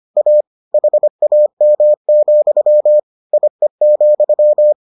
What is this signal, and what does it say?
Morse code, heard as a keyed tone, about 25 wpm, 600 Hz. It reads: A HAM, IE,